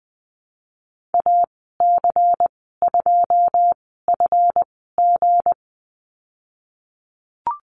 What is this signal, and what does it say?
Morse code, heard as a keyed tone, about 20 wpm, spaced as for 15 wpm, 700 Hz. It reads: AC2FG
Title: 186 Lesson 30g - Letter b - Call Signs 20wpm